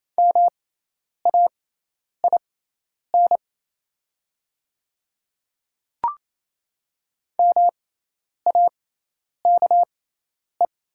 Morse code, heard as a keyed tone, about 28 wpm, spaced as for 10 wpm, 690 Hz. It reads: MAIN MAKE
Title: 6x Character Spacing - Top 500 Words - 28wpm